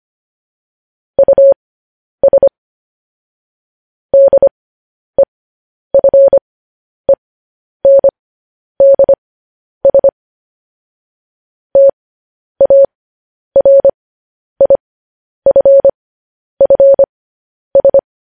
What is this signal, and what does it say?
Morse code, heard as a keyed tone, about 25 wpm, spaced as for 10 wpm, 550 Hz. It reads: US DEFENDS TARIFFS
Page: news headlines in morse code at 10 wpm